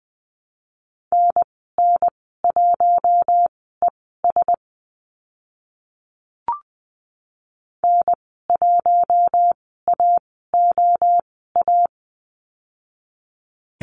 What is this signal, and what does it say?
Morse code, heard as a keyed tone, about 20 wpm, spaced as for 14 wpm, 700 Hz. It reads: NN1ES N1AOA
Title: 042 Lesson 8g - Number 1 - Call Signs 20wpm